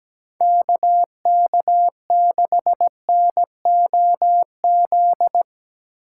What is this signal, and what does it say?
Morse code, heard as a keyed tone, about 18 wpm, 700 Hz. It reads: KK6NOZ
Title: Speed Racing - USA 2x3 Call Signs 17wpm